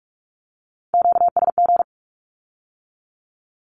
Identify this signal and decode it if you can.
Morse code, heard as a keyed tone, about 46 wpm, 700 Hz. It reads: QSG
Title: Q-Codes with Spoken Meaning - 45wpm